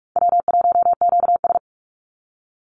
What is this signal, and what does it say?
Morse code, heard as a keyed tone, about 44 wpm, 700 Hz. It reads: W1QS
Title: Call Signs Repeated 3x - 45wpm